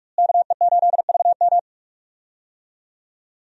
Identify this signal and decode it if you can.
Morse code, heard as a keyed tone, about 45 wpm, 700 Hz. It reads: KE8VM
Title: USA Call Signs - 5 Characters - Repeated 3x - 45wpm